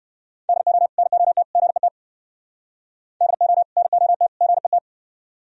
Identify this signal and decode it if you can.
Morse code, heard as a keyed tone, about 58 wpm, 700 Hz. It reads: DO NOT GET DO NOT GET